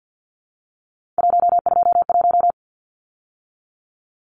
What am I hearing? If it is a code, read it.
Morse code, heard as a keyed tone, about 50 wpm, 700 Hz. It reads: 121
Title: Numbers - 3-Digit 50wpm